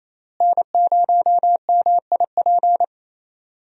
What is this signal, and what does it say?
Morse code, heard as a keyed tone, about 28 wpm, 700 Hz. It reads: N0MIP
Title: USA Call Signs - 5 Characters - Repeated 2x - 28wpm